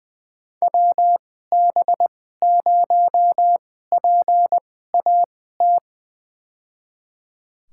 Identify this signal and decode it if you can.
Morse code, heard as a keyed tone, about 20 wpm, spaced as for 14 wpm, 700 Hz. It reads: WB0PAT
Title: USA Call Signs - 6 Characters - 2x Character Spacing - 20wpm